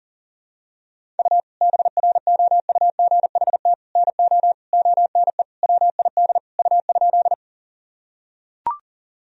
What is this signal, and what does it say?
Morse code, heard as a keyed tone, about 40 wpm, 700 Hz. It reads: U BROUGHT NO ONE WID U?
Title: Sentences from Top 500 Words and Abbreviations 40wpm